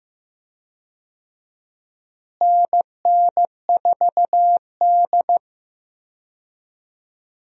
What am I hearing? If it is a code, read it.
Morse code, heard as a keyed tone, about 15 wpm, 700 Hz. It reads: NN4D
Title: USA Call Signs - 4 Characters - Repeated 3x - 15wpm